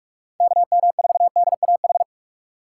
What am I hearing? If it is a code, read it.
Morse code, heard as a keyed tone, about 59 wpm, 700 Hz. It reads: KM4DAH